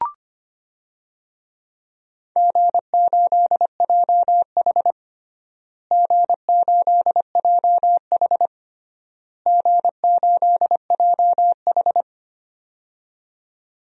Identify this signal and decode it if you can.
Morse code, heard as a keyed tone, about 25 wpm, 700 Hz. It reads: G8JH G8JH G8JH
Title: DXCC-100 Call Signs - 4 Characters - Repeated 3x - 25wpm